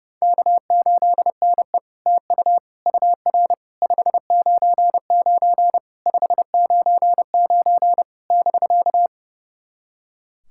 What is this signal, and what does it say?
Morse code, heard as a keyed tone, about 32 wpm, 700 Hz. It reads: K8NE TU UR 599 599 <BK>
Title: SOTA QSO Practice 30wpm